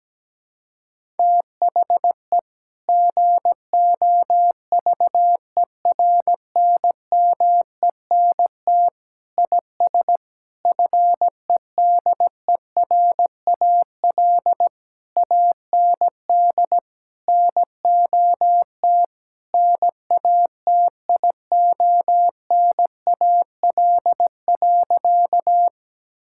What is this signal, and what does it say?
Morse code, heard as a keyed tone, about 17 wpm, 700 Hz. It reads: THE GOVERNMENT IS FEDERAL AND NOT NATIONAL.
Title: Sentences from Top 1000 Words - Review Part 5 - 17wpm